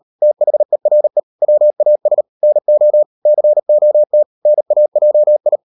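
Morse code, heard as a keyed tone, about 38 wpm, 600 Hz. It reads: THERE WAS NO COT NAJI
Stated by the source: Morse code practice at 38 WPM, sending Ken Follett's novel Never in English, from CW med Gnister.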